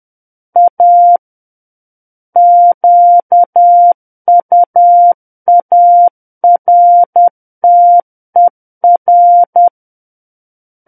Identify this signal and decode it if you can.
Morse code, heard as a keyed tone, about 10 wpm, 700 Hz. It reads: A QUARTER
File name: Book One / 01-Chapter 1 - 10 WPM